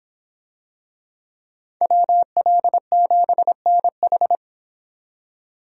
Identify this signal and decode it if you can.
Morse code, heard as a keyed tone, about 26 wpm, 700 Hz. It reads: WL7NH